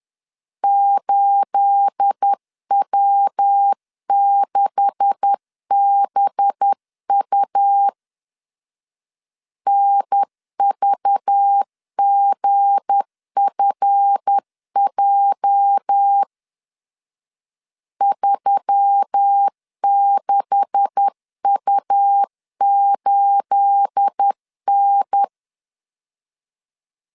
Morse code, heard as a keyed tone, about 10 wpm, 800 Hz. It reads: 8W6BU NVGFJ 36U8N
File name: LEZIONI / 9 286 10wpm 2735